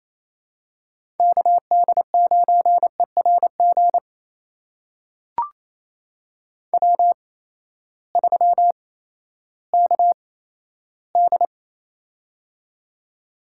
Morse code, heard as a keyed tone, about 28 wpm, spaced as for 14 wpm, 700 Hz. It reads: KD9ERG W3KD
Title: USA Call Signs - 8x Character Spacing - 28wpm